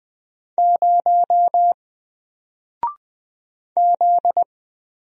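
Morse code, heard as a keyed tone, about 20 wpm, 700 Hz. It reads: 0 Z
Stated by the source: Single letter and number Morse drill at 20 wpm, from N4LCD.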